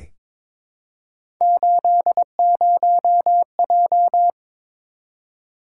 Morse code, heard as a keyed tone, about 22 wpm, 700 Hz. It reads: 80J
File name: practice / 22wpm / Three Letter-Number Combinations - 22wpm